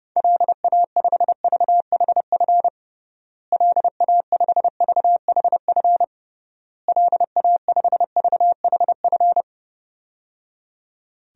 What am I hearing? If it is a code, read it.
Morse code, heard as a keyed tone, about 30 wpm, 700 Hz. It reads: LA5VHF LA5VHF LA5VHF
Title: DXCC-100 Call Signs - 6 Characters - Repeated 3x - 30wpm